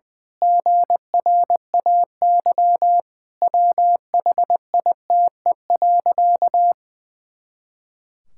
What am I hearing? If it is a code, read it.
Morse code, heard as a keyed tone, about 21 wpm, 700 Hz. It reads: GRAY WHITE.
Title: Pangram Challenge v2 - 20wpm